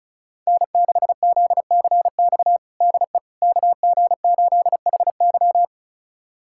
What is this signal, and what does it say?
Morse code, heard as a keyed tone, about 34 wpm, 700 Hz. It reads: N6ZCX DE KG8HY
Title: USA Callsign DE Callsign 35wpm